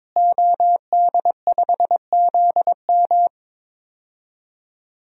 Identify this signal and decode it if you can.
Morse code, heard as a keyed tone, about 22 wpm, 700 Hz. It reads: OD5ZM
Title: DXCC-100 Call Signs - 5 Characters - Repeated 3x - 22wpm